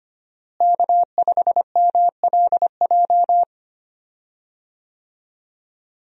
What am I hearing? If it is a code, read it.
Morse code, heard as a keyed tone, about 25 wpm, 700 Hz. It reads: K5MLJ
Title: USA Call Signs - 5 Characters - Repeated 2x - 25wpm